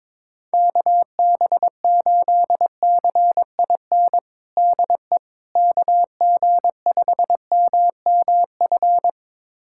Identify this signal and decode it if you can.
Morse code, heard as a keyed tone, about 21 wpm, 700 Hz. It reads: KB8CIN DE KG5MMF
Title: USA Callsign DE Callsign - 6 Characters - 22wpm